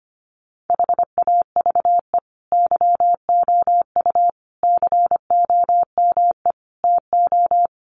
Morse code, heard as a keyed tone, about 25 wpm, 700 Hz. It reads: HAVE YOU COME TO
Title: Sentences from Top 1400 Words 25wpm